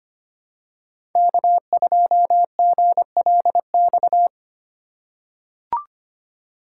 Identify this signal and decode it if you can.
Morse code, heard as a keyed tone, about 25 wpm, 700 Hz. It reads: K2GLX
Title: USA Call Signs - 5 Characters - 4x Character Spacing - 25wpm